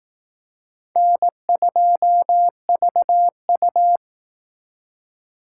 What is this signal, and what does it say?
Morse code, heard as a keyed tone, about 18 wpm, 700 Hz. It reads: N2VU
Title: Speed Racing - USA Call Signs 15wpm